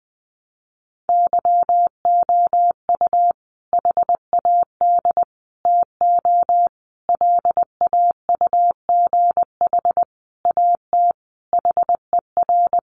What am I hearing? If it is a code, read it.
Morse code, heard as a keyed tone, about 20 wpm, 700 Hz. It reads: YOU HAD TO LAUGH AT HER
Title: Sentences from Top 600 Words 20wpm